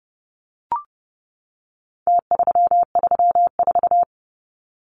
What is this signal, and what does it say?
Morse code, heard as a keyed tone, about 30 wpm, 700 Hz. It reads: T334
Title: Numbers - Contest Style - 4-Digit 30wpm